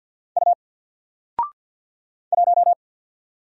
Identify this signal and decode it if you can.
Morse code, heard as a keyed tone, about 50 wpm, 700 Hz. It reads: U 1
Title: Single Letter-Number 50wpm